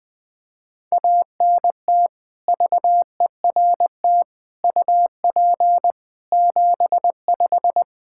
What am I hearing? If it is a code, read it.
Morse code, heard as a keyed tone, about 20 wpm, 700 Hz. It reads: ANT VERT UP 75
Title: K1USN SST QSO Practice 20wpm